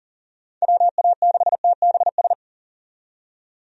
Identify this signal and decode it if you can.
Morse code, heard as a keyed tone, about 40 wpm, 700 Hz. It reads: WA6TBS
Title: USA Call Signs - 6 Characters - 40wpm